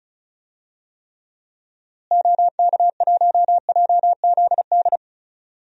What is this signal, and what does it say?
Morse code, heard as a keyed tone, about 34 wpm, 700 Hz. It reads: OK1JZD
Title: DXCC-100 Call Signs - 6 Characters - Repeated 2x - 35wpm